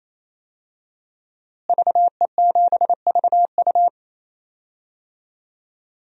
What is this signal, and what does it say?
Morse code, heard as a keyed tone, about 27 wpm, 700 Hz. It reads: VE7VU
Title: DXCC-100 Call Signs - 5 Characters - Repeated 3x - 28wpm